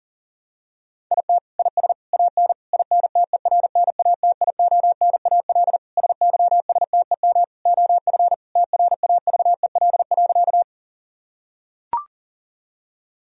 Morse code, heard as a keyed tone, about 40 wpm, 700 Hz. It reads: IT IS AN INTERNATIONAL SYSTEM OF TRAVEL.
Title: Sentences from Top 1000 Words - Review Part 4 - 40wpm